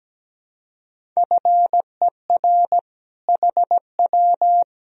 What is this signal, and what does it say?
Morse code, heard as a keyed tone, about 17 wpm, 700 Hz. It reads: FER HW
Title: Sentences from Top 200 Words and Abbreviations 17wpm